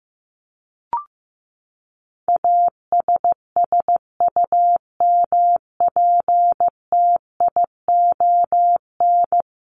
Morse code, heard as a keyed tone, about 15 wpm, 700 Hz. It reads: ASSUMPTION
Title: Top 1751-2000 Words 15wpm